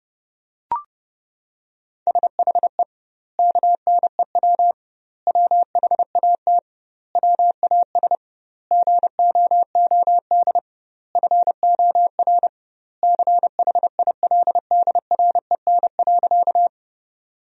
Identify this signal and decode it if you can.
Morse code, heard as a keyed tone, about 30 wpm, 700 Hz. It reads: SHE KNEW WHAT WAS GOOD FOR CHILDREN.
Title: 2x Word-Spacing - Sentences from Top 400 Words - 30wpm